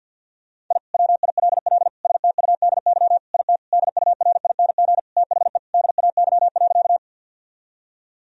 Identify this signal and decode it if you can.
Morse code, heard as a keyed tone, about 50 wpm, 700 Hz. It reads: I WILL STUDY IT DURING THE DAY.
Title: Sentences from Top 1000 Words - Review Part 2 - 50wpm